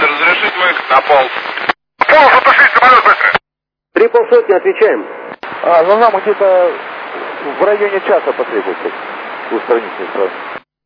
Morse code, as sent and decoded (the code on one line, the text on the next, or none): none